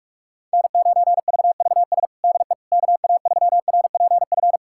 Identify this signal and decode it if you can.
Morse code, heard as a keyed tone, about 46 wpm, 690 Hz. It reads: N9VVS DE KA3RPF
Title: USA Callsign DE Callsign 45wpm